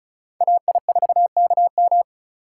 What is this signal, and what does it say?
Morse code, heard as a keyed tone, about 34 wpm, 700 Hz. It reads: AI4KM
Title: USA Call Signs - 5 Characters - 4x Character Spacing - 35wpm